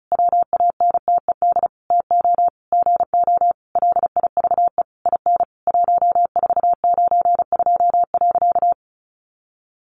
Morse code, heard as a keyed tone, about 34 wpm, 700 Hz. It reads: WANTED TO GO LIVE IN 1492.
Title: A Sound Of Thunder 35wpm